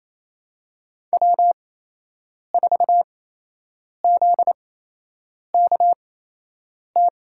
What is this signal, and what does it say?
Morse code, heard as a keyed tone, about 28 wpm, spaced as for 8 wpm, 700 Hz. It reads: W4ZKT
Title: USA Call Signs - 8x Character Spacing - 28wpm